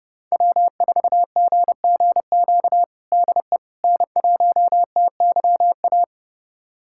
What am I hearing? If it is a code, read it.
Morse code, heard as a keyed tone, about 30 wpm, 700 Hz. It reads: W4GGQ DE N1TYA
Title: USA Callsign DE Callsign - 5 Characters - 30wpm